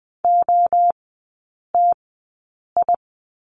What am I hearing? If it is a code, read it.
Morse code, heard as a keyed tone, about 20 wpm, spaced as for 8 wpm, 700 Hz. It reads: OTI
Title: 047 Lesson 9e - Number 4 - Three Characters 20wpm